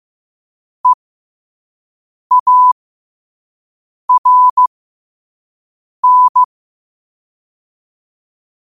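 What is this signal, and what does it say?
Morse code, heard as a keyed tone, about 15 wpm, spaced as for 5 wpm, 1000 Hz. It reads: EARN
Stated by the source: Morse code sent at 5 words per minute.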